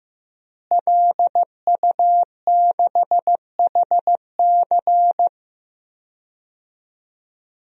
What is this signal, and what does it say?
Morse code, heard as a keyed tone, about 15 wpm, 700 Hz. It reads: LU6HC
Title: DXCC-100 Call Signs - 5 Characters - Repeated 2x - 15wpm